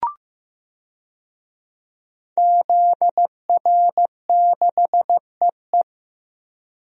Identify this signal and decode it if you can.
Morse code, heard as a keyed tone, about 15 wpm, 700 Hz. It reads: ZR6EE
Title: DXCC-100 Call Signs - 5 Characters - Repeated 2x - 15wpm